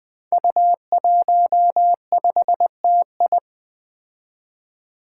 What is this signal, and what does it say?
Morse code, heard as a keyed tone, about 21 wpm, 700 Hz. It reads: U15TI
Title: Five Challenging Letter-Number Combinations 20wpm